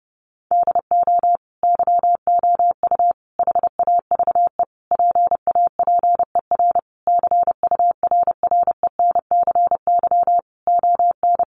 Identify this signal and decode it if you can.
Morse code, heard as a keyed tone, about 30 wpm, 700 Hz. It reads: DO YOU HAVE PAPER CURRENCY ON
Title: Sentences from Top 1500 Words 30wpm